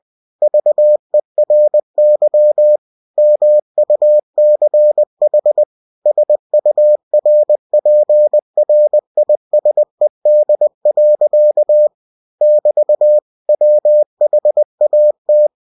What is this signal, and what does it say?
Morse code, heard as a keyed tone, about 20 wpm, 600 Hz. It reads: VERY MUCH SURPRISED. <BT> WHAT
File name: cw-books / winnie-the-pooh / winnie-20wpm-page-0014